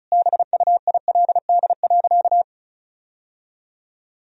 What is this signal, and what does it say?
Morse code, heard as a keyed tone, about 36 wpm, 700 Hz. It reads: BUILD.
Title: Sentences from Top 1000 Words - Review Part 1 - 35wpm